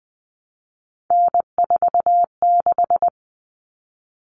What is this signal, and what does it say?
Morse code, heard as a keyed tone, about 19 wpm, 700 Hz. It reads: N46
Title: Common Cut Numbers - 3-Digit 20wpm